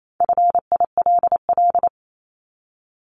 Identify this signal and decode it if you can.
Morse code, heard as a keyed tone, about 28 wpm, 700 Hz. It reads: FILL
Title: Top 500 Words Repeated 28wpm